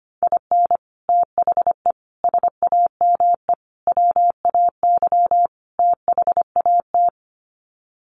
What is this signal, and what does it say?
Morse code, heard as a keyed tone, about 25 wpm, 700 Hz. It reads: IN THE SAME WAY THAT
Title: Sets of 5 Words 25wpm